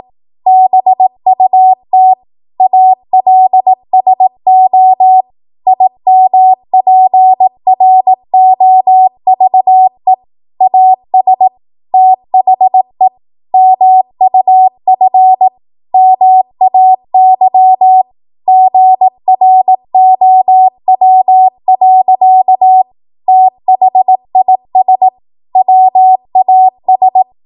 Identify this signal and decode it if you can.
Morse code, heard as a keyed tone, about 18 wpm, 750 Hz. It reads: BUT ALSO IMPROVE AS THE MUF MAY GROW. THIS WAS